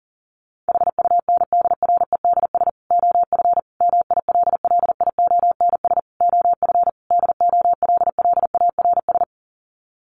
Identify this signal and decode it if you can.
Morse code, heard as a keyed tone, about 40 wpm, 700 Hz. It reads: HUNDREDS OF MILLIONS OF DOLLARS
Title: Sets of 5 Words 40wpm